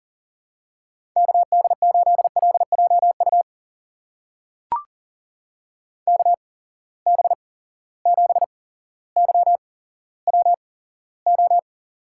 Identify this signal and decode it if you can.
Morse code, heard as a keyed tone, about 40 wpm, spaced as for 17 wpm, 700 Hz. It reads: KD8LJU KB7YWO
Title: USA Call Signs - 8x Character Spacing - 40wpm